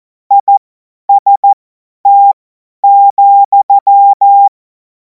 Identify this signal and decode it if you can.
Morse code, heard as a keyed tone, about 14 wpm, spaced as for 10 wpm, 800 Hz. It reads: IST,
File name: Morsekurs / Witze / Witz 03. März 2026 50 F70